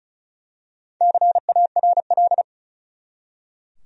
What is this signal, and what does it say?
Morse code, heard as a keyed tone, about 36 wpm, 700 Hz. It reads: CARL
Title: USA Common Names - Max 5 Letters - 35wpm